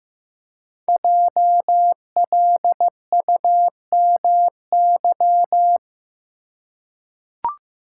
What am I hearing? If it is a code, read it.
Morse code, heard as a keyed tone, about 15 wpm, 700 Hz. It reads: JLUMY
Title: Code Groups - 5 Letters - 15wpm